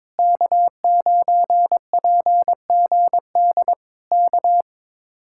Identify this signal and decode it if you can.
Morse code, heard as a keyed tone, about 22 wpm, 700 Hz. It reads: K9PGD K
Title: Speed-Racing Common QSO Phrases - 22wpm